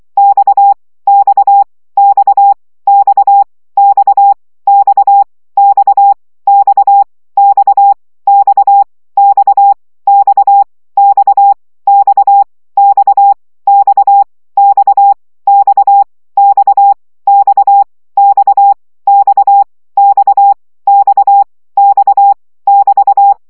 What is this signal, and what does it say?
Morse code, heard as a keyed tone, about 24 wpm, 800 Hz. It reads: X X X X X X X X X X X X X X X X X X X X X X X X X <BT>
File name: CW gyakorlas x 20 baud